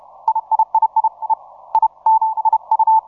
Morse code, heard as a keyed tone, about 31 wpm, 880 Hz. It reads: IIII E I N IU
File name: simple-file-list / FT5ZM 20MCW